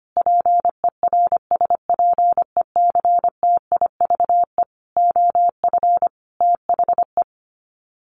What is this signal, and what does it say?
Morse code, heard as a keyed tone, about 25 wpm, 700 Hz. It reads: PERSPECTIVE OF THE
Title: Sets of 5 Words Encore2 25wpm